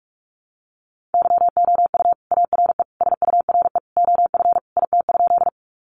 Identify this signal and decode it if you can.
Morse code, heard as a keyed tone, about 44 wpm, 700 Hz. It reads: YOU ARE SURE OF IT?
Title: Sentences from Top 300 Words 45wpm